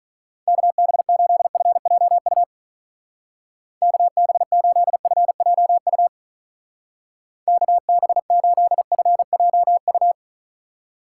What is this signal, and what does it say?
Morse code, heard as a keyed tone, about 44 wpm, 700 Hz. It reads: KB8FJU KB8FJU KB8FJU